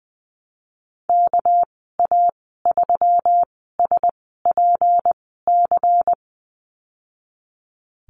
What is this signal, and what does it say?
Morse code, heard as a keyed tone, about 20 wpm, spaced as for 15 wpm, 700 Hz. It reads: KA3SPC
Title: 238 Lesson 38g - Letter q - Call Signs 20wpm